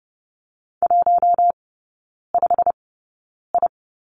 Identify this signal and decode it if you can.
Morse code, heard as a keyed tone, about 30 wpm, spaced as for 9 wpm, 700 Hz. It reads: 15I
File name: Lesson 16e - Letter c - Three Characters 30wpm